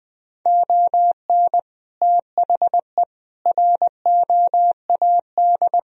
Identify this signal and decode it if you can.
Morse code, heard as a keyed tone, about 20 wpm, 700 Hz. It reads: ON THE ROAD
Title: Sentences from Top 1000 Words - Review Part 2 - 20wpm